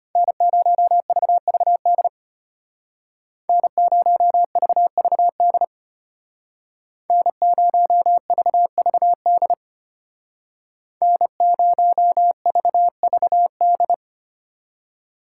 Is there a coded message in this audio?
Yes: Morse code from 0.2 to 13.9 s, heard as a keyed tone, about 33 wpm, 700 Hz. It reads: N0VVD N0VVD N0VVD N0VVD